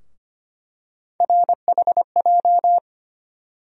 Morse code, heard as a keyed tone, about 25 wpm, 700 Hz. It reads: RHJ
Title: USA Call Sign Postfixes - 3 Letters 25wpm